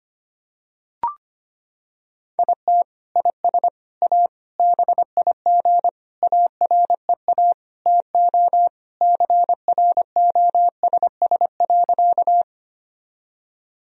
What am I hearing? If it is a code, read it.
Morse code, heard as a keyed tone, about 25 wpm, 700 Hz. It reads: IT IS A BIG AREA TO CROSS.
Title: Sentences from Top 1000 Words - Review Part 2 - 25wpm